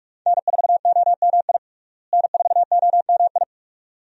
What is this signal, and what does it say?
Morse code, heard as a keyed tone, about 42 wpm, 700 Hz. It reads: N4OMI N4OMI
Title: USA Call Signs - 5 Characters - Repeated 2x - 45wpm